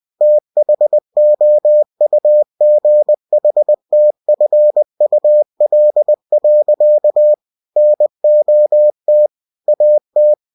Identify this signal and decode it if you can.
Morse code, heard as a keyed tone, about 20 wpm, 600 Hz. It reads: THOUGHTFUL. NOT AT